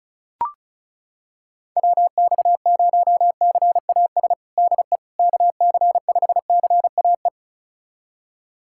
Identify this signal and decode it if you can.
Morse code, heard as a keyed tone, about 34 wpm, 690 Hz. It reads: WX0CAS DE KC5CAE
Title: USA Callsign DE Callsign - 6 Characters - 35wpm